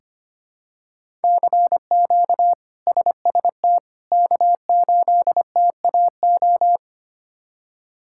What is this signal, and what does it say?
Morse code, heard as a keyed tone, about 25 wpm, 700 Hz. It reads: CQ SST K8TAO
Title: K1USN SST QSO Practice - 25wpm